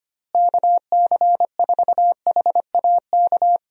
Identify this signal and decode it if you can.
Morse code, heard as a keyed tone, about 25 wpm, 700 Hz. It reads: KC4HAK
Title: USA Call Signs - 6 Characters - Repeated 2x - 25wpm